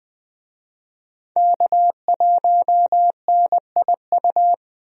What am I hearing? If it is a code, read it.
Morse code, heard as a keyed tone, about 21 wpm, 700 Hz. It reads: K1NIU
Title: USA Call Signs - 5 Characters - Repeated 2x - 20wpm